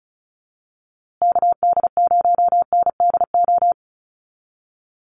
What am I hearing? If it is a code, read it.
Morse code, heard as a keyed tone, about 34 wpm, 700 Hz. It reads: KD0NDO
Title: USA Call Signs Repeated 3x 35wpm